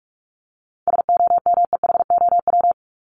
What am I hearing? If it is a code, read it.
Morse code, heard as a keyed tone, about 44 wpm, 700 Hz. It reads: SOMEHOW
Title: Top 1501-1750 Words 45wpm